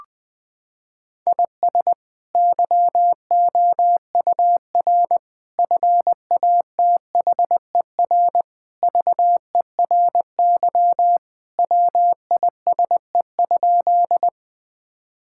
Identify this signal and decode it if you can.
Morse code, heard as a keyed tone, about 20 wpm, 700 Hz. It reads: IS YOUR FATHER VERY WISE?
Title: Sentences from Top 1600 Words 20wpm